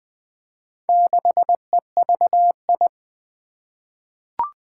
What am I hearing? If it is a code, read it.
Morse code, heard as a keyed tone, about 20 wpm, 700 Hz. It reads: 6EVI
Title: Four Challenging Letter-Number Combinations 20wpm